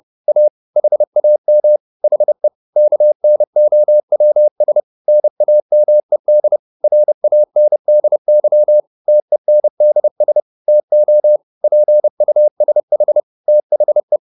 Morse code, heard as a keyed tone, about 30 wpm, 600 Hz. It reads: A HAM HE KNOWS NAMED RANDY TENDS TO PUSH THE